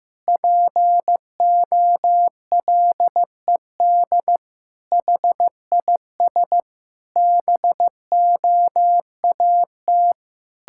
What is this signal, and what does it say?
Morse code, heard as a keyed tone, about 16 wpm, 700 Hz. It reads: POLED HIS BOAT
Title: Pangram Challenge v2 - 15wpm